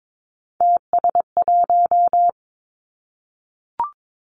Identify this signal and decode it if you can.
Morse code, heard as a keyed tone, about 22 wpm, 700 Hz. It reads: TS1
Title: Three Challenging Letter-Number Combinations 22wpm